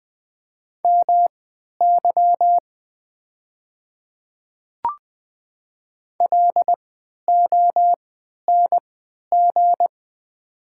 Morse code, heard as a keyed tone, about 20 wpm, spaced as for 11 wpm, 700 Hz. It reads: MY LONG
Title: 3x Character Spacing - Top 200 Words - 20wpm